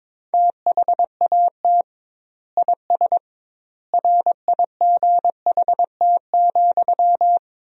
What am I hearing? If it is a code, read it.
Morse code, heard as a keyed tone, about 22 wpm, 700 Hz. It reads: THAT IS RIGHT,
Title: 2x Word-Spacing - Sentences from Top 200 Words - 22wpm